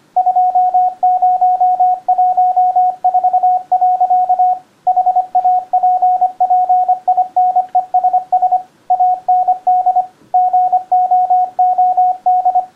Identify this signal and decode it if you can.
Morse code, heard as a keyed tone, about 25 wpm, 730 Hz, with radio noise. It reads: 2014. HAPPINESS AND GOOD